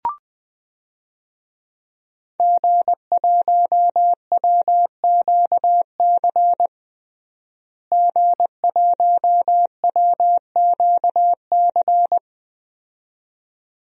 Morse code, heard as a keyed tone, about 20 wpm, 700 Hz. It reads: G1WQC G1WQC
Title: DXCC-100 Call Signs - 5 Characters - Repeated 2x - 20wpm